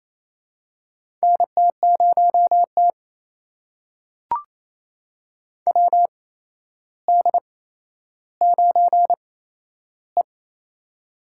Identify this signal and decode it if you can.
Morse code, heard as a keyed tone, about 28 wpm, spaced as for 12 wpm, 700 Hz. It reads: NT0T WD9E
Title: USA Call Signs - 8x Character Spacing - 28wpm